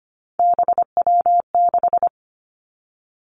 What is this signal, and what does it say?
Morse code, heard as a keyed tone, about 25 wpm, 700 Hz. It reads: BW6
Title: Three Challenging Letter-Number Combinations 25wpm